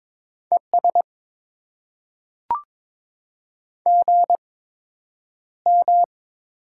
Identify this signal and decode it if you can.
Morse code, heard as a keyed tone, about 22 wpm, spaced as for 9 wpm, 700 Hz. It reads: ES GM
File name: Top 20 QSO Elements - 8x Character Spacing - 22wpm